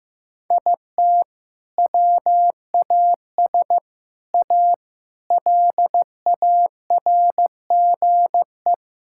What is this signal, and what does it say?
Morse code, heard as a keyed tone, about 15 wpm, 700 Hz. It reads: IT WAS A LARGE
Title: Sentences from Top 1600 Words 15wpm